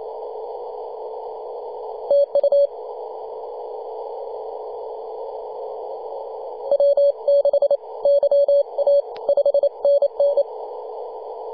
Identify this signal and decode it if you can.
Morse code, heard as a keyed tone, about 28 wpm, spaced as for 25 wpm, 570 Hz. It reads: TU W6YA 5NN
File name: 2015/May/15 2222z PW0F 21.026MHz CW